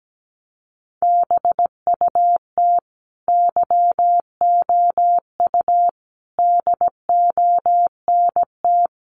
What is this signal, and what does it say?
Morse code, heard as a keyed tone, about 17 wpm, 700 Hz. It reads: BUT YOU DONT
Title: Sentences from Top 600 Words 17wpm